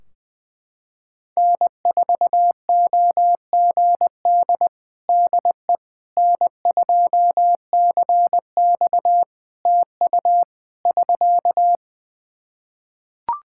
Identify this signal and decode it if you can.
Morse code, heard as a keyed tone, about 20 wpm, 700 Hz. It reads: N4OGD DE N2CX TU <SK>
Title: Practice QSO - Exchange 4 - Ending - 20wpm